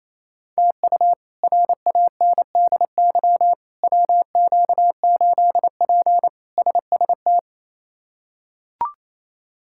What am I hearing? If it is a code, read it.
Morse code, heard as a keyed tone, about 28 wpm, 700 Hz. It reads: TU RANDY WQ8P SST
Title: K1USN SST QSO Practice - 28wpm